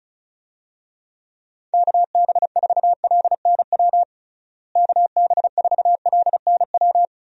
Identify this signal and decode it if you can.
Morse code, heard as a keyed tone, about 35 wpm, 700 Hz. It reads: KB4LNW KB4LNW
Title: USA Call Signs - 6 Characters - Repeated 2x - 35wpm